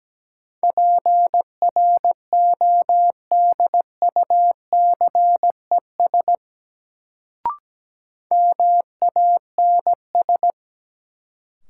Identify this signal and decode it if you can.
Morse code, heard as a keyed tone, about 17 wpm, 700 Hz. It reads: PRODUCES MANS
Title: Sentences from Top 1000 Words - Review Part 6 - New Word Drill - 17wpm